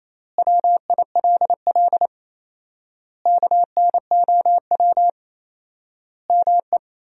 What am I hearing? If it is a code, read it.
Morse code, heard as a keyed tone, about 28 wpm, 700 Hz. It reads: WILL KNOW ME
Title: 4x Word-Spacing - Sentences from Top 400 Words - 28wpm